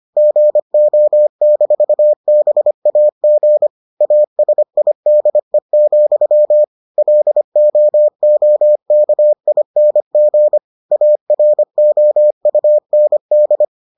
Morse code, heard as a keyed tone, about 25 wpm, 600 Hz. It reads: GO-BAG ASIDE, LOOKING AROUND